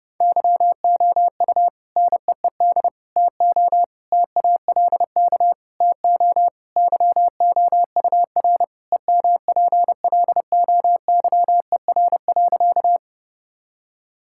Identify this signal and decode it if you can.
Morse code, heard as a keyed tone, about 30 wpm, 700 Hz. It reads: YOU NEED TO TALK TO YOUR EMPLOYER.
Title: Sentences from Top 1600 Words 30wpm